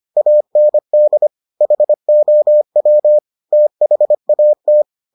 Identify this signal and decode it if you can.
Morse code, heard as a keyed tone, about 25 wpm, 600 Hz. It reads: AND HOW THAT